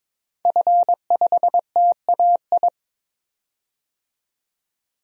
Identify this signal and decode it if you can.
Morse code, heard as a keyed tone, about 22 wpm, 700 Hz. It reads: F5TAI
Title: DXCC-100 Call Signs - 5 Characters - 22wpm